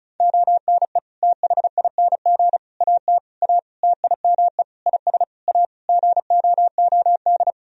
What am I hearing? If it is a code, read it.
Morse code, heard as a keyed tone, about 34 wpm, 700 Hz. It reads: ONE THING AT A TIME IS A GOOD
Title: Sentences from Top 1000 Words - Review Part 2 - 35wpm